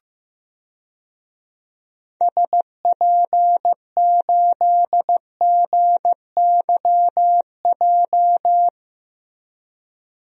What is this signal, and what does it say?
Morse code, heard as a keyed tone, about 15 wpm, 700 Hz. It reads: SP8GYJ
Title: DXCC-100 Call Signs - 6 Characters - Repeated 2x - 15wpm